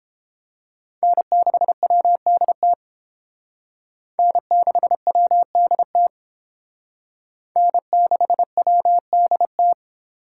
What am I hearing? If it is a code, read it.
Morse code, heard as a keyed tone, about 28 wpm, 700 Hz. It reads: N6WDT N6WDT N6WDT